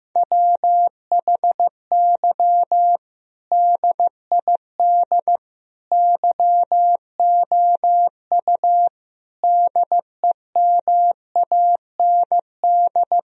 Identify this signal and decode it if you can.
Morse code, heard as a keyed tone, about 15 wpm, 700 Hz. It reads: WHY DID YOU DEMAND
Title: Sentences from Top 1000 Words - Review Part 4 - 15wpm